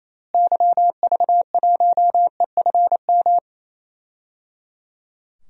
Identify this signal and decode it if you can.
Morse code, heard as a keyed tone, about 27 wpm, 700 Hz. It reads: YV1EFM
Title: DXCC-100 Call Signs - 6 Characters - Repeated 3x - 28wpm